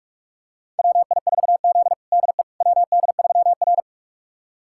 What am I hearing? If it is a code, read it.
Morse code, heard as a keyed tone, about 41 wpm, 700 Hz. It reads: WI4Z DE WD3R
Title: USA Callsign DE Callsign - 4 Characters - 45wpm